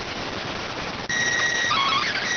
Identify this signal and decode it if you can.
SSTV (slow-scan television), PD 290.